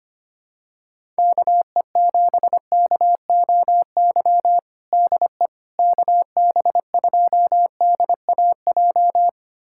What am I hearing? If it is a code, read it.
Morse code, heard as a keyed tone, about 25 wpm, 700 Hz. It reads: KE7KOY DE KB2DAJ
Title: USA Callsign DE Callsign - 6 Characters - 25wpm